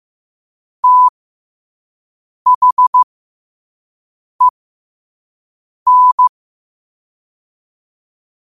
Morse code, heard as a keyed tone, about 15 wpm, spaced as for 5 wpm, 1000 Hz. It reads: THEN
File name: qotdfiles / 5wpm